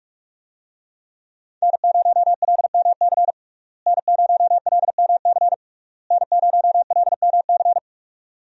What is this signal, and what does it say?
Morse code, heard as a keyed tone, about 43 wpm, 700 Hz. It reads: N0LMC N0LMC N0LMC
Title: USA Call Signs - 5 Characters - Repeated 3x - 45wpm